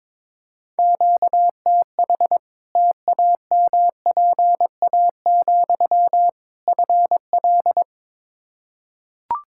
Morse code, heard as a keyed tone, about 22 wpm, 700 Hz. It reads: QTH TAMPA, FL
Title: Practice QSO - Exchange 1 - Beginning - 22wpm